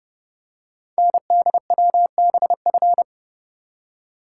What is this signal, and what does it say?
Morse code, heard as a keyed tone, about 28 wpm, 690 Hz. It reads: NDWBF
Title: Code Groups - 5 Letters - 30wpm